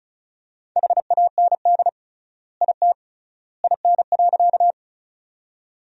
Morse code, heard as a keyed tone, about 34 wpm, 700 Hz. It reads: HAND IT IN.
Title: 3x Word-Spacing - Sentences from Top 200 Words - 35wpm